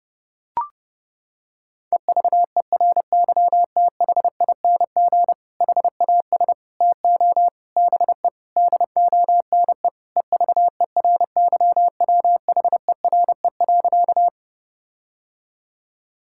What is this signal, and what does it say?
Morse code, heard as a keyed tone, about 30 wpm, 700 Hz. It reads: EVERYTHING HAS TO BE DONE EVERYWHERE.
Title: Sentences from Top 1600 Words 30wpm